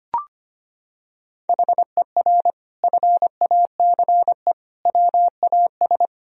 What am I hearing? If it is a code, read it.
Morse code, heard as a keyed tone, about 25 wpm, 700 Hz. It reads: HER FACE WAS
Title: Sentences from Top 1000 Words - Review Part 2 - 25wpm